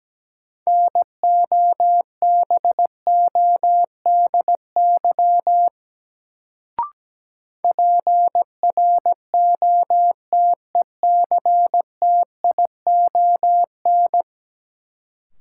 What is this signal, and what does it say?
Morse code, heard as a keyed tone, about 17 wpm, 700 Hz. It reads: NOBODY PROTECTION
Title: Sentences from Top 1600 Words - New Word Drill - 17wpm